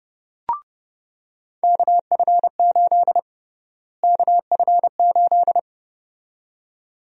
Morse code, heard as a keyed tone, about 30 wpm, 700 Hz. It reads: KF8 KF8
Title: USA Call Sign Prefixes - Repeated 2x - 30wpm